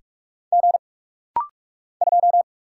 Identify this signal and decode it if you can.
Morse code, heard as a keyed tone, about 44 wpm, 700 Hz. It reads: G 2
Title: Single Letter-Number - Rapid-Fire 45wpm